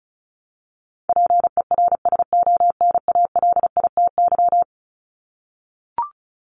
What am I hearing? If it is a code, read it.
Morse code, heard as a keyed tone, about 34 wpm, 700 Hz. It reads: PERSONALITY
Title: Top 1251-1500 Words 35wpm